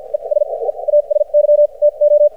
Morse code, heard as a keyed tone, about 54 wpm, spaced as for 38 wpm, 590 Hz, with radio noise. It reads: ES EASY TO
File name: archived / CRSnet